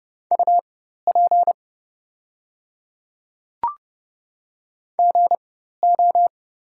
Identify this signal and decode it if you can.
Morse code, heard as a keyed tone, about 30 wpm, spaced as for 14 wpm, 700 Hz. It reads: UP GO
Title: 4x Character Spacing - Top 200 Words - 30wpm